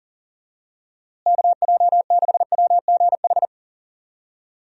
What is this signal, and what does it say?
Morse code, heard as a keyed tone, about 40 wpm, 700 Hz. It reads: KJ6WGH
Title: USA Call Signs - 6 Characters - Repeated 2x - 40wpm